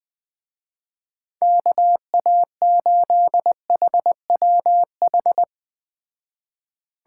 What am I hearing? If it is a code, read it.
Morse code, heard as a keyed tone, about 19 wpm, 700 Hz. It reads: KA8HWH